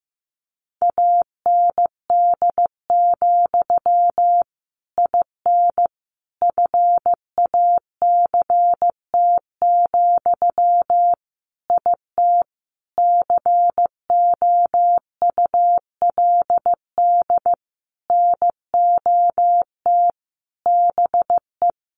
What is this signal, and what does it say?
Morse code, heard as a keyed tone, about 15 wpm, 700 Hz. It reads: AND, IN FACT, IT COULD NOT BE
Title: Sentences from Top 1200 Words 15wpm